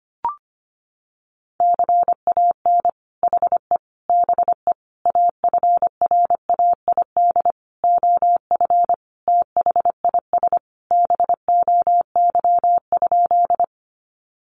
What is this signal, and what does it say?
Morse code, heard as a keyed tone, about 25 wpm, 700 Hz. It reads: CAN HE BE AFRAID OF THIS BOY?
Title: Sentences from Top 1500 Words 25wpm